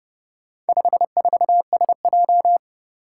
Morse code, heard as a keyed tone, about 30 wpm, 700 Hz. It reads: 54SJ
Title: Four Challenging Letter-Number Combinations 30wpm